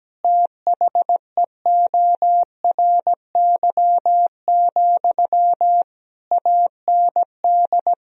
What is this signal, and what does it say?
Morse code, heard as a keyed tone, about 17 wpm, 700 Hz. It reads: THEORY, AND
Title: Sentences from Top 1000 Words - Review Part 4 - 17wpm